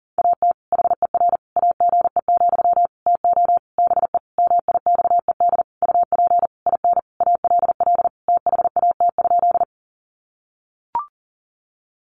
Morse code, heard as a keyed tone, about 40 wpm, 700 Hz. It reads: AT HER AGE, TO BE MIXED UP IN ALL THAT?
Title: Sentences from Top 1500 Words 40wpm